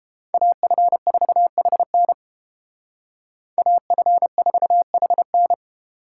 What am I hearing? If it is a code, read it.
Morse code, heard as a keyed tone, about 31 wpm, 700 Hz. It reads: AF4HN AF4HN